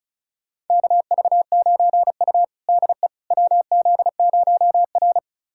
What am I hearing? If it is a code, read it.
Morse code, heard as a keyed tone, about 34 wpm, 700 Hz. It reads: KV9U DE WZ0R
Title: USA Callsign DE Callsign - 4 Characters - 35wpm